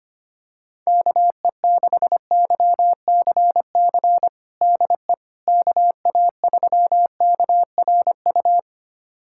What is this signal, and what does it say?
Morse code, heard as a keyed tone, about 25 wpm, 700 Hz. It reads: KE6YCC DE KA3KRU
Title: USA Callsign DE Callsign 25wpm